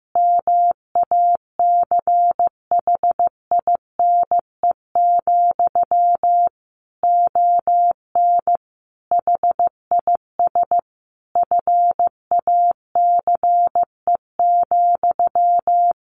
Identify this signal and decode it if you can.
Morse code, heard as a keyed tone, about 15 wpm, 700 Hz. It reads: MACHINE, ON HIS FACE,
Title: A Sound Of Thunder 15wpm